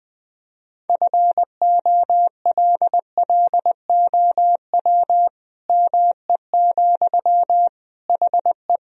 Morse code, heard as a keyed tone, about 20 wpm, 700 Hz. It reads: FOLLOW ME, HE
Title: Sentences from Top 1000 Words - Review Part 1 - 20wpm